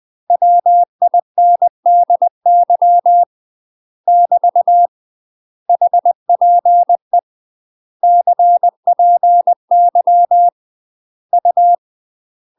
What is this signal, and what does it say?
Morse code, heard as a keyed tone, about 20 wpm, 700 Hz. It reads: WINDY <BT> HPE CPY U
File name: UK MASTER QSOs TRAINING / 20WPM / UK MASTER QSOs 0011